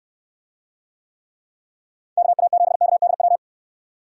Morse code, heard as a keyed tone, about 69 wpm, 700 Hz. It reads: KA7KDW